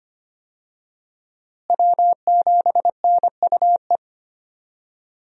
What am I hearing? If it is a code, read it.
Morse code, heard as a keyed tone, about 25 wpm, 700 Hz. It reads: W7NUE
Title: USA Call Signs - 5 Characters - Repeated 2x - 25wpm